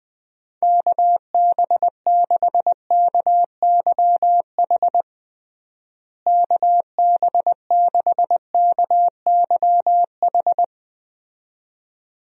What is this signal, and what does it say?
Morse code, heard as a keyed tone, about 20 wpm, 700 Hz. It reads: KB6KYH KB6KYH
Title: USA Call Signs - 6 Characters - Repeated 2x - 20wpm